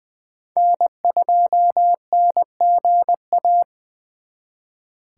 Morse code, heard as a keyed tone, about 24 wpm, 700 Hz. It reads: N2NGA
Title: Speed Racing - USA 1x3 Call Signs 20wpm